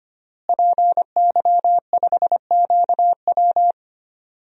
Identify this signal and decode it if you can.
Morse code, heard as a keyed tone, about 25 wpm, 700 Hz. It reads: PY5QW
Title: DXCC-100 Call Signs - 5 Characters - Repeated 3x - 25wpm